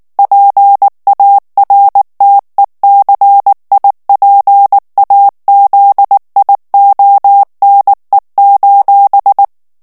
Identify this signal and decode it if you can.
Morse code, heard as a keyed tone, about 19 wpm, 800 Hz. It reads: PARTECIPAZIONE:
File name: morse mp3 / ITALIEN texte clair 2 vitesse 18 mots minute